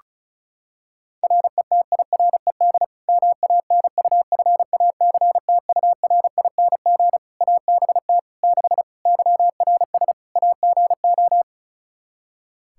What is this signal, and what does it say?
Morse code, heard as a keyed tone, about 34 wpm, 700 Hz. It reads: RETIRED MANUFACTURING ABT 6 YRS AGO
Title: Practice QSO - Exchange 3 - More Info - 35wpm